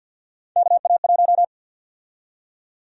Morse code, heard as a keyed tone, about 52 wpm, 700 Hz. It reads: KA1
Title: USA Call Sign Prefixes - 50wpm